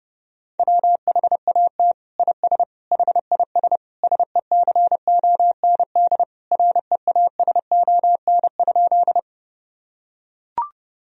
Morse code, heard as a keyed tone, about 30 wpm, 700 Hz. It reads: WHAT IS HIS SECOND REASON?
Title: Sentences from Top 1000 Words - Review Part 1 - 30wpm